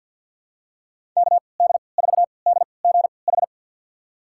Morse code, heard as a keyed tone, about 47 wpm, spaced as for 28 wpm, 700 Hz. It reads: KD4DGH
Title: USA Call Signs - 6 Characters - 3x Character Spacing - 50wpm